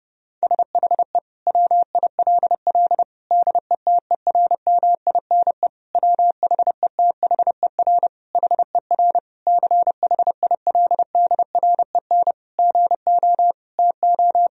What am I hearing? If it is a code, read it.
Morse code, heard as a keyed tone, about 30 wpm, 700 Hz. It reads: SHE WILL DETERMINE WHETHER HER CHILDREN GO TO
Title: Sentences from Top 1000 Words - Review Part 4 - 30wpm